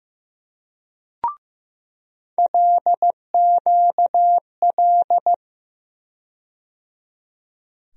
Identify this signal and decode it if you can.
Morse code, heard as a keyed tone, about 15 wpm, 700 Hz. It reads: LQL
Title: USA Call Sign Postfixes - 3 Letters 15wpm